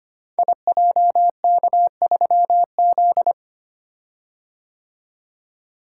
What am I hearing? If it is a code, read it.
Morse code, heard as a keyed tone, about 25 wpm, 700 Hz. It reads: IJK3Z
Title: Code Groups - 5 Letter:Numbers - 25wpm